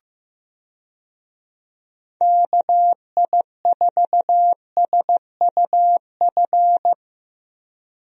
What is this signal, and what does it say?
Morse code, heard as a keyed tone, about 15 wpm, 700 Hz. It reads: KI4SUF
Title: USA Call Signs - 6 Characters - Repeated 2x - 15wpm